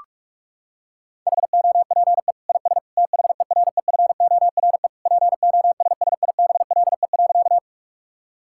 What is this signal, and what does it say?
Morse code, heard as a keyed tone, about 46 wpm, 700 Hz. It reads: HOPE IS THEREFORE POSSIBLE.
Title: Sentences from Top 1000 Words - Review Part 5 - 45wpm